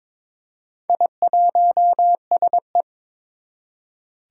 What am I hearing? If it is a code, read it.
Morse code, heard as a keyed tone, about 22 wpm, 700 Hz. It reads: I1SE
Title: Four Challenging Letter-Number Combinations 22wpm